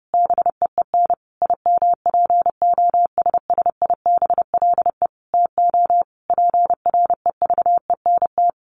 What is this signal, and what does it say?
Morse code, heard as a keyed tone, about 30 wpm, 700 Hz. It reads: BEEN IMPOSSIBLE TO PREVENT
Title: Sentences from Top 1100 Words 30wpm